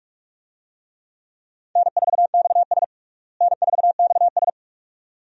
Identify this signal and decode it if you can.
Morse code, heard as a keyed tone, about 47 wpm, 700 Hz. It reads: N4XS N4XS
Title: USA Call Signs - 4 Characters - Repeated 2x - 45wpm